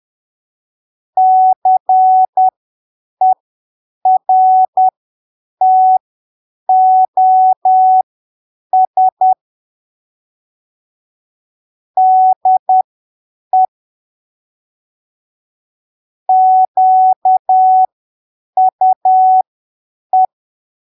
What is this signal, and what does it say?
Morse code, heard as a keyed tone, about 10 wpm, spaced as for 7 wpm, 750 Hz. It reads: CERTOS DE QUE
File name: arquivos / texto / texto 15 10ppm